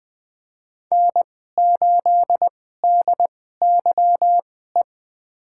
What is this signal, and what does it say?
Morse code, heard as a keyed tone, about 20 wpm, spaced as for 15 wpm, 700 Hz. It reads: N8DYE
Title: USA Call Signs - 5 Characters - 2x Character Spacing - 20wpm